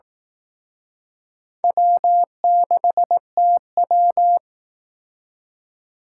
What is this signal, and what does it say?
Morse code, heard as a keyed tone, about 18 wpm, 700 Hz. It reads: W6TW
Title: Speed Racing - USA 1x2 Call Signs 15wpm